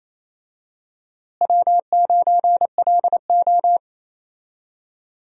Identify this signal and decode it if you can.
Morse code, heard as a keyed tone, about 28 wpm, 700 Hz. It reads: W9LO
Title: USA Call Signs - 4 Characters - Repeated 2x - 28wpm